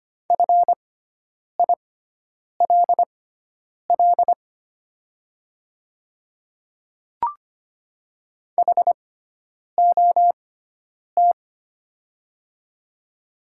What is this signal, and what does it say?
Morse code, heard as a keyed tone, about 25 wpm, spaced as for 9 wpm, 700 Hz. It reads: FILL HOT
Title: 6x Character Spacing - Top 500 Words - 25wpm